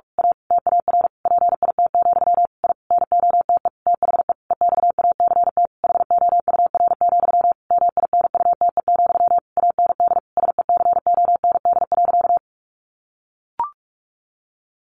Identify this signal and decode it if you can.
Morse code, heard as a keyed tone, about 44 wpm, 700 Hz. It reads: A TAR PIT, I NOTE THE EXACT HOUR, MINUTE, AND SECOND.
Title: A Sound Of Thunder 45wpm